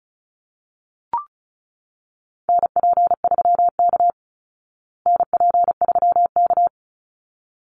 Morse code, heard as a keyed tone, about 34 wpm, 700 Hz. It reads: NP3K NP3K
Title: Call Signs Repeated 3x - 35wpm